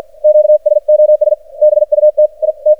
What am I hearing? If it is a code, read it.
Morse code, heard as a keyed tone, about 53 wpm, 610 Hz. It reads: QSOS BUT I T
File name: archived / CRSnet